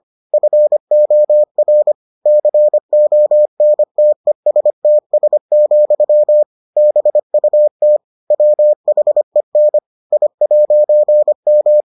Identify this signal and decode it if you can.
Morse code, heard as a keyed tone, about 25 wpm, 600 Hz. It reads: FOR CONTESTS, BUT WHEN I'M